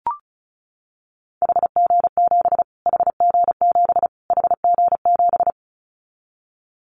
Morse code, heard as a keyed tone, about 34 wpm, 700 Hz. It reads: HG7 HG7 HG7
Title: Callsign Prefixes for Top 25 DXCC - Repeated 3x - 35wpm